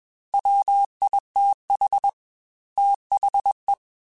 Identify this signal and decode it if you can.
Morse code, heard as a keyed tone, about 21 wpm, 800 Hz. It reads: WITH THE
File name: Technical Subjects / Technical Subjects-mp3s / XC @20